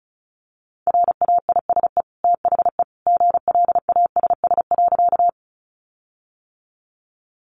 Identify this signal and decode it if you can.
Morse code, heard as a keyed tone, about 34 wpm, 700 Hz. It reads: RAISE THE GLASS.
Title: Sentences from Top 1400 Words 35wpm